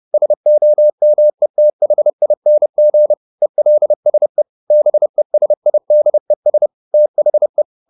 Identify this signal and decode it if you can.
Morse code, heard as a keyed tone, about 30 wpm, 600 Hz. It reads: SOMETHING ELSE BESIDES THE